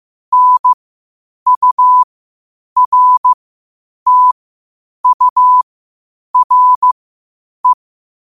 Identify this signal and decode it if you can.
Morse code, heard as a keyed tone, about 15 wpm, spaced as for 8 wpm, 1000 Hz. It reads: NURTURE
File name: qotdfiles / 8wpm